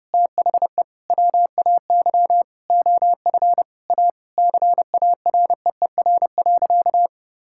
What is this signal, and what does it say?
Morse code, heard as a keyed tone, about 30 wpm, 700 Hz. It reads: THE WAY OF A CAREER.
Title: Sentences from Top 1000 Words - Review Part 6 - 30wpm